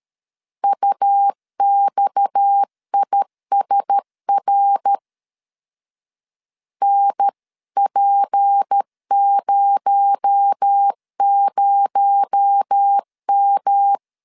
Morse code, heard as a keyed tone, about 13 wpm, 800 Hz. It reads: UXISR NP00M
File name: CW / DET / 1 6 1500C S B 12wpm 3431